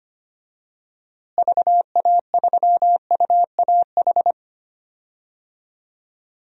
Morse code, heard as a keyed tone, about 25 wpm, 700 Hz. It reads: VA3UAH
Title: DXCC-100 Call Signs - 6 Characters - Repeated 3x - 25wpm